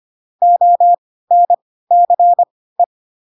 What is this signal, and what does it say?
Morse code, heard as a keyed tone, about 25 wpm, spaced as for 16 wpm, 700 Hz. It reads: ONCE
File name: Pinocchio / Pinocchio ch1 10 25 ik1hge